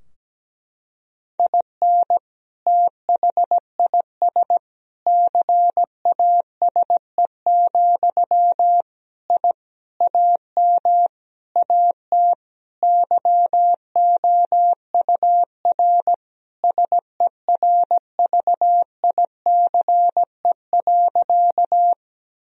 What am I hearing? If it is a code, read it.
Morse code, heard as a keyed tone, about 17 wpm, 700 Hz. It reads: IN THIS CASE, I AM AT YOUR SERVICE.
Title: Sentences from Top 1000 Words - Review Part 4 - 17wpm